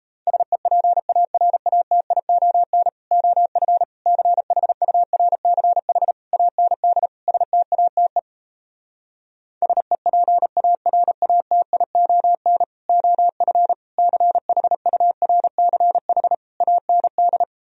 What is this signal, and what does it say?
Morse code, heard as a keyed tone, about 38 wpm, 700 Hz. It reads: SEPARATION OF CHURCH AND STATE SEPARATION OF CHURCH AND